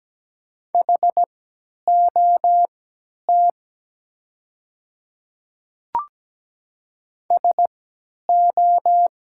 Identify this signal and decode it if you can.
Morse code, heard as a keyed tone, about 17 wpm, spaced as for 10 wpm, 700 Hz. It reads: HOT SO
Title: 3x Character Spacing - Top 100 Words - 17wpm